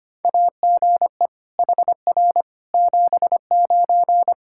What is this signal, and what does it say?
Morse code, heard as a keyed tone, about 25 wpm, 700 Hz. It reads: AGE HR 79
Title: Practice QSO - Exchange 3 - More Info - 25wpm